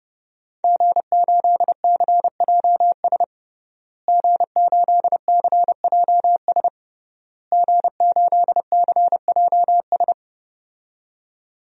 Morse code, heard as a keyed tone, about 32 wpm, 700 Hz. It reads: G8CJS G8CJS G8CJS
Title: DXCC-100 Call Signs - 5 Characters - Repeated 3x - 30wpm